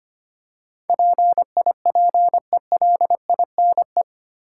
Morse code, heard as a keyed tone, about 25 wpm, 700 Hz. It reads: PIPELINE
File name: Eight Letter Words - 25wpm